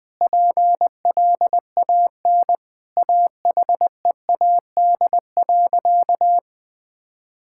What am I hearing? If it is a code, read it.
Morse code, heard as a keyed tone, about 20 wpm, 700 Hz. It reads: PLAN AHEAD.
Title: Sentences with 7-Word Phrases - Spoken Beginning - 20wpm